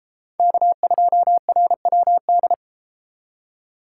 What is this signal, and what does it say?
Morse code, heard as a keyed tone, about 29 wpm, 700 Hz. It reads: K2RWD